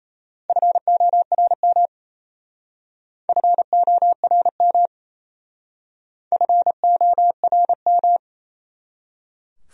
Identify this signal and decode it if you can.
Morse code, heard as a keyed tone, about 37 wpm, 700 Hz. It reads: FORM FORM FORM